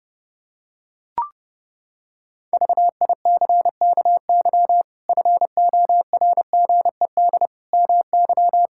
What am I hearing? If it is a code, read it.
Morse code, heard as a keyed tone, about 30 wpm, 700 Hz. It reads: VICKY FORGED MY
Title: Pangram Challenge v2 - 30wpm